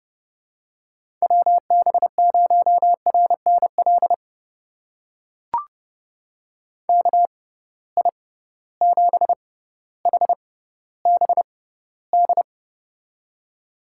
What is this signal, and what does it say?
Morse code, heard as a keyed tone, about 30 wpm, spaced as for 15 wpm, 700 Hz. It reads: WB0RNL KI7HBD
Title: USA Call Signs - 6 Characters - 6x Character Spacing - 30wpm